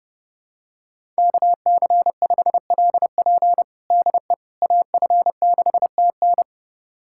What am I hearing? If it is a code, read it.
Morse code, heard as a keyed tone, about 30 wpm, 700 Hz. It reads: KC5LP DE AF6TN
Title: USA Callsign DE Callsign - 5 Characters - 30wpm